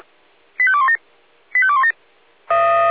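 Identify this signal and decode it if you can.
Five-tone selective call (ZVEI), heard as quick runs of five tones, tones of about 80 ms. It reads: 85217 85217
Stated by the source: ZVEI five-tone selective call 85244 received on an SDR.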